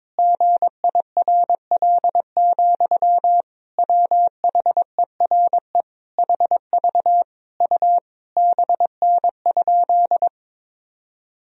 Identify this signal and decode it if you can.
Morse code, heard as a keyed tone, about 22 wpm, 700 Hz. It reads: GIRL, WHERE HV U BN?
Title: Sentences from Top 400 Words and Abbreviations 22wpm